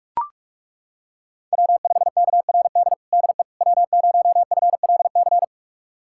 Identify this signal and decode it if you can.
Morse code, heard as a keyed tone, about 41 wpm, 700 Hz. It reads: W5KRD DE W0FLC
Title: USA Callsign DE Callsign - 5 Characters - 45wpm